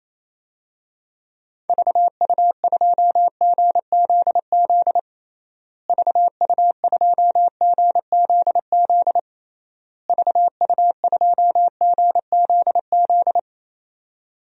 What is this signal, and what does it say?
Morse code, heard as a keyed tone, about 28 wpm, 700 Hz. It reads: VU2GZZ VU2GZZ VU2GZZ
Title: DXCC-100 Call Signs - 6 Characters - Repeated 3x - 28wpm